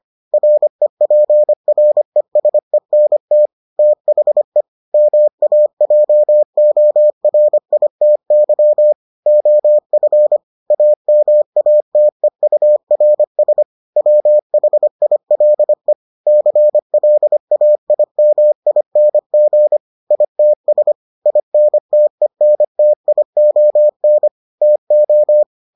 Morse code, heard as a keyed tone, about 25 wpm, 600 Hz. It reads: REPRESENT THE MAJORITY OF AMATEURS WHILE CLAIMING ITS INTENTION TO